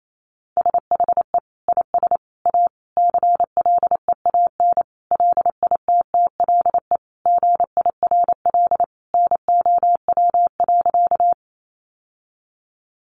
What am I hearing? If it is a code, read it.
Morse code, heard as a keyed tone, about 28 wpm, 700 Hz. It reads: SHE IS A CLEAN LITTLE GIRL NOW.
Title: Sentences from Top 1100 Words 28wpm